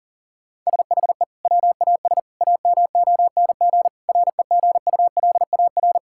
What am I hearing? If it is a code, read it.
Morse code, heard as a keyed tone, about 40 wpm, 700 Hz. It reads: SHE WAS AMONG REGULAR
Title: Sentences from Top 1000 Words - Review Part 6 - 40wpm